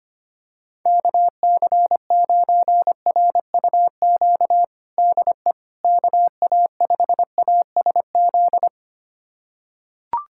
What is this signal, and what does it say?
Morse code, heard as a keyed tone, about 25 wpm, 700 Hz. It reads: KC9RUQ DE KA5ASZ
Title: USA Callsign DE Callsign - 6 Characters - 25wpm